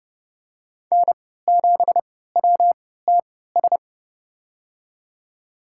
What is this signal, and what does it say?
Morse code, heard as a keyed tone, about 32 wpm, spaced as for 17 wpm, 700 Hz. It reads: N7WTS
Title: USA Call Signs - 5 Characters - 3x Character Spacing - 30wpm